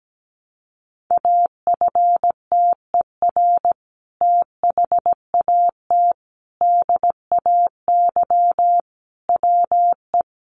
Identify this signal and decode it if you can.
Morse code, heard as a keyed tone, about 17 wpm, 700 Hz. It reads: AFTER THAT DAY WE
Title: Sentences from Top 1100 Words 17wpm